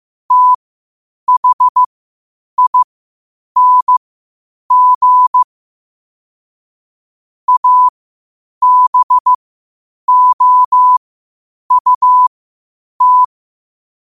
Morse code, heard as a keyed tone, about 15 wpm, spaced as for 8 wpm, 1000 Hz. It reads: THING ABOUT
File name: qotdfiles / 8wpm